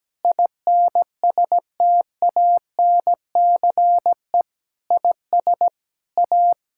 Morse code, heard as a keyed tone, about 17 wpm, 700 Hz. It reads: INSTANCE IS A
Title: Sentences from Top 1000 Words - Review Part 5 - 17wpm